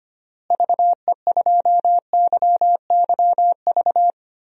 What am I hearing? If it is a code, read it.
Morse code, heard as a keyed tone, about 25 wpm, 700 Hz. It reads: VE2YYV
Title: DXCC-100 Call Signs - 6 Characters - Repeated 2x - 25wpm